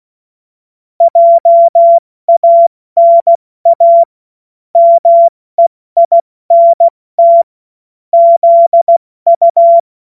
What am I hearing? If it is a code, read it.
Morse code, heard as a keyed tone, about 16 wpm, 680 Hz. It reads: JANA MEINT ZU
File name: Morsekurs / Witze / Witz 16. Mai 2023 70 F80